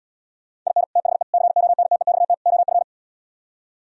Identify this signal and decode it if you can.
Morse code, heard as a keyed tone, about 77 wpm, 700 Hz. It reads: IN THE CONTEXT OF